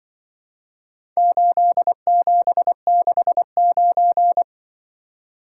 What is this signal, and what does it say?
Morse code, heard as a keyed tone, about 24 wpm, 700 Hz. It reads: 8769
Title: Speed-Racing - Four Letter-Number Combinations - 20wpm